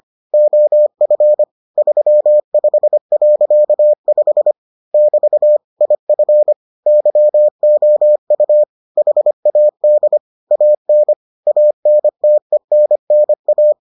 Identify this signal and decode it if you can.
Morse code, heard as a keyed tone, about 25 wpm, 600 Hz. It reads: OF 35.5 <BT> IF YOU HAD AN ANTENNA